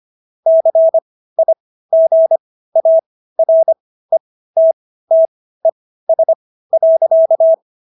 Morse code, heard as a keyed tone, about 25 wpm, spaced as for 15 wpm, 650 Hz. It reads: CIGARETTES.